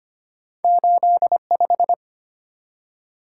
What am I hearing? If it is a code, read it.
Morse code, heard as a keyed tone, about 25 wpm, 690 Hz. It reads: 85
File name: Two Letter-Number Combinations - 25wpm